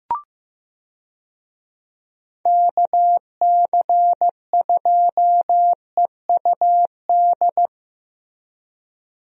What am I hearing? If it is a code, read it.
Morse code, heard as a keyed tone, about 16 wpm, 700 Hz. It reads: KC2EUD
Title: USA Call Signs - 6 Characters - Repeated 2x - 15wpm